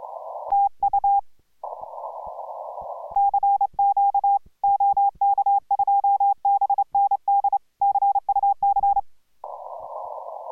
Morse code, heard as a keyed tone, about 28 wpm, 800 Hz. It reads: TU CQ OK2BND CUC